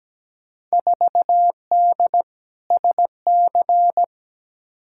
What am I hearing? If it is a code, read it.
Morse code, heard as a keyed tone, about 17 wpm, 700 Hz. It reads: 4D SC
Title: Field Day Exchanges Encore - 17wpm